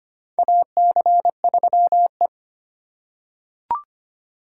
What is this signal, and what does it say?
Morse code, heard as a keyed tone, about 25 wpm, 700 Hz. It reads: AC3E
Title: USA Call Signs - 4 Characters - 3x Character Spacing - 25wpm